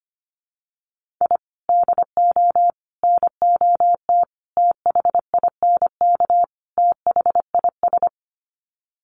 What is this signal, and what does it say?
Morse code, heard as a keyed tone, about 25 wpm, 700 Hz. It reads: I DO NOT THINK THIS
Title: Sets of 5 Words Encore 25wpm